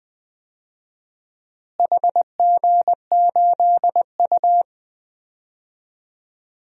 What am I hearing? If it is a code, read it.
Morse code, heard as a keyed tone, about 20 wpm, 700 Hz. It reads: HG8U
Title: DXCC-100 Call Signs - 4 Characters - Repeated 3x - 20wpm